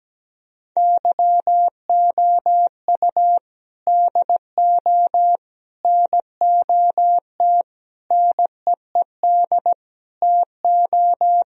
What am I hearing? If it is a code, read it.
Morse code, heard as a keyed tone, about 17 wpm, 700 Hz. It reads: YOU DO NOT NEED TO
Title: Speed-Racing - Sets of 5 Words 17wpm